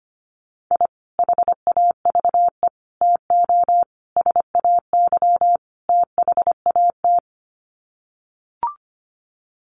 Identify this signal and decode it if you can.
Morse code, heard as a keyed tone, about 25 wpm, 700 Hz. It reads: I HAVE TO SAY THAT
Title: Sets of 5 Words Encore 25wpm